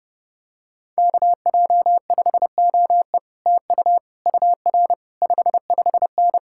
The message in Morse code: KJ5OE TU UR 55N

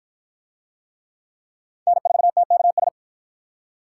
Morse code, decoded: N4TKS

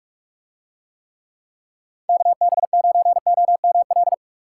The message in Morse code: KB9OML

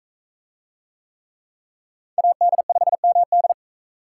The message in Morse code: AD5MD

